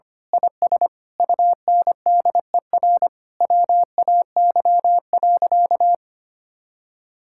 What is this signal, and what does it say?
Morse code, heard as a keyed tone, about 25 wpm, 700 Hz. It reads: IS UNDER WAY.
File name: practice / 25wpm / Sentences with 9-Word Phrases - 25wpm